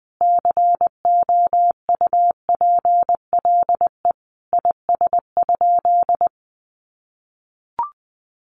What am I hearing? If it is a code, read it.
Morse code, heard as a keyed tone, about 20 wpm, 700 Hz. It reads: COUPLE IS?
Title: Sentences from Top 1100 Words 20wpm